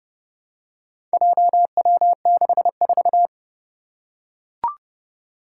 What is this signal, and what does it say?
Morse code, heard as a keyed tone, about 30 wpm, 700 Hz. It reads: JW64
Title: Four Challenging Letter-Number Combinations 30wpm